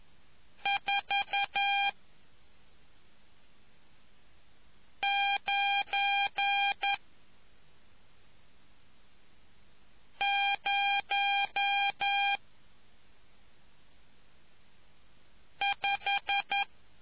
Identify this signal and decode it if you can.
Morse code, heard as a keyed tone, about 11 wpm, 800 Hz. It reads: 4 9 0 5